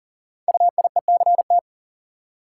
Morse code, heard as a keyed tone, about 41 wpm, 690 Hz. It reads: UIECT
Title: Code Groups - 5 Letters - 40wpm